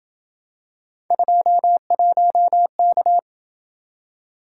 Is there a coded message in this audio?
Yes: Morse code from 1.1 to 3.2 s, heard as a keyed tone, about 27 wpm, 700 Hz. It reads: 21K